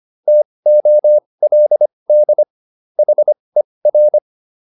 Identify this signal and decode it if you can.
Morse code, heard as a keyed tone, about 25 wpm, spaced as for 20 wpm, 600 Hz. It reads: TOLD HER